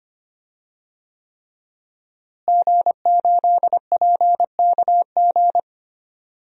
Morse code, heard as a keyed tone, about 25 wpm, 700 Hz. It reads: G8PKG
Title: DXCC-100 Call Signs - 5 Characters - Repeated 3x - 25wpm